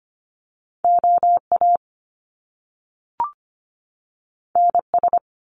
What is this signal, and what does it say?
Morse code, heard as a keyed tone, about 25 wpm, 700 Hz. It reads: OA NS
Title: Top Two Letter Combinations in Words 25wpm